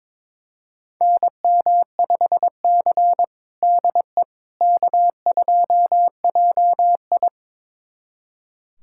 Morse code, heard as a keyed tone, about 21 wpm, 700 Hz. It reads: NM5C DE K2JI
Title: USA Callsign DE Callsign - 4 Characters - 22wpm